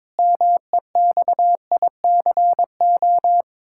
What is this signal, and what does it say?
Morse code, heard as a keyed tone, about 22 wpm, 700 Hz. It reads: MEXICO
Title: Pangram Challenge v2 - 22wpm